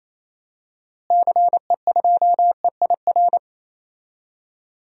Morse code, heard as a keyed tone, about 28 wpm, 700 Hz. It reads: CE2EIR
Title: DXCC-100 Call Signs - 6 Characters - Repeated 3x - 28wpm